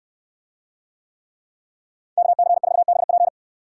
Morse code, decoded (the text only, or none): KC4BP